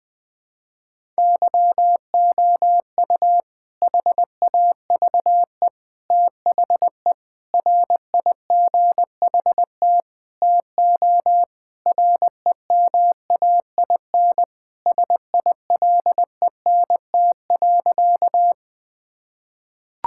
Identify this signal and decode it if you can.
Morse code, heard as a keyed tone, about 21 wpm, 700 Hz. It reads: YOU HAVE THE RIGHT TO REMAIN SILENT.
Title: Sentences with 4-Word Phrases - 20wpm